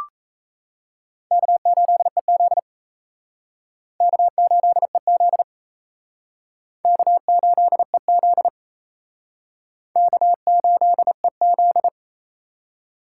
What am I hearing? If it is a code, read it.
Morse code, heard as a keyed tone, about 38 wpm, 700 Hz. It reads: K8EZ K8EZ K8EZ K8EZ